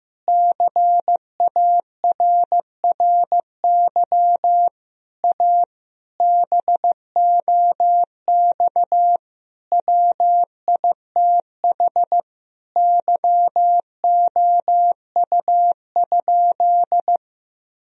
Morse code, heard as a keyed tone, about 15 wpm, 700 Hz. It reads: CARRY A BOX WITH YOU?
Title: Sentences from Top 1000 Words - Review Part 2 - 15wpm